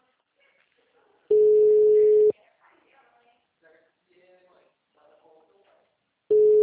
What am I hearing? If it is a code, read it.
Ringback tone (Europe, 425 Hz), with speech.